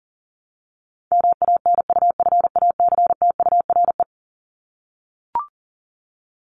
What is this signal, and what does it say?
Morse code, heard as a keyed tone, about 40 wpm, 700 Hz. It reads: MANUFACTURE
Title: ING Suffix - Top 100 Words with Base Words 40wpm